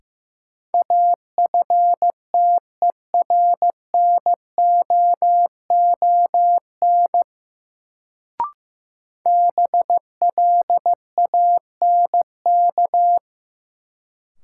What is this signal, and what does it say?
Morse code, heard as a keyed tone, about 15 wpm, 700 Hz. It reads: AFTERNOON BLANK
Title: Sentences from Top 1500 Words - New Word Drill Alt - 15wpm